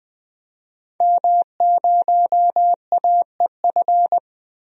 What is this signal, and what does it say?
Morse code, heard as a keyed tone, about 20 wpm, 700 Hz. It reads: M0AEF
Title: DXCC-100 Call Signs - 5 Characters - Repeated 2x - 20wpm